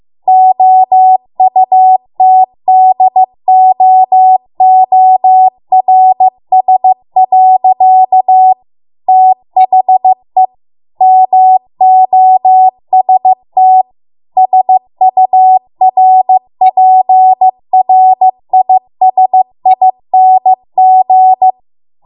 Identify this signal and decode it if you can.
Morse code, heard as a keyed tone, about 15 wpm, 750 Hz. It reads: OUTDOORS. THE MOST SURPRISING